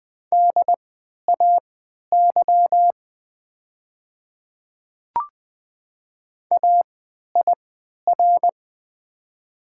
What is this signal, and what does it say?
Morse code, heard as a keyed tone, about 20 wpm, spaced as for 11 wpm, 690 Hz. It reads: DAY AIR
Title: 3x Character Spacing - Top 200 Words - 20wpm